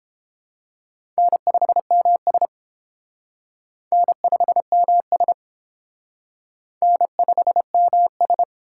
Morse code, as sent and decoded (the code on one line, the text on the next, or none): -. ..... -- ... / -. ..... -- ... / -. ..... -- ...
N5MS N5MS N5MS